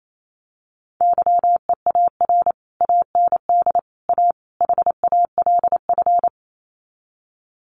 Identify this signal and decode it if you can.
Morse code, heard as a keyed tone, about 28 wpm, 700 Hz. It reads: YEAR AND A HALF
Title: Sets of 4 Words 28wpm